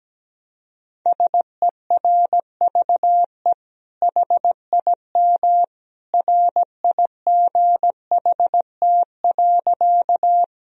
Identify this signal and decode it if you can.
Morse code, heard as a keyed tone, about 17 wpm, 700 Hz. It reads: SERVE HIM RIGHT.
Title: Sentences from Top 1000 Words - Review Part 2 - 17wpm